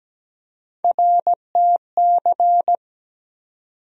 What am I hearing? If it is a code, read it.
Morse code, heard as a keyed tone, about 17 wpm, 700 Hz. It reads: RTC
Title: USA Call Sign Postfixes - 3 Letters 17wpm